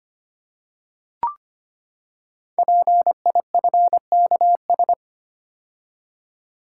Morse code, heard as a keyed tone, about 25 wpm, 700 Hz. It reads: PIFKS
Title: Code Groups - 5 Letters - 25wpm